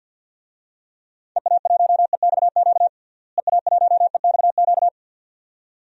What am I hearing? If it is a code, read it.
Morse code, heard as a keyed tone, about 47 wpm, 700 Hz. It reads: EA1EXQ EA1EXQ
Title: DXCC-100 Call Signs - 6 Characters - Repeated 2x - 50wpm